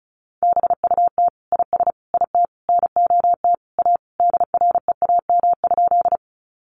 Morse code, heard as a keyed tone, about 34 wpm, 700 Hz. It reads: BUT IS IT NOT A DREAM?
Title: Sentences from Top 1500 Words 35wpm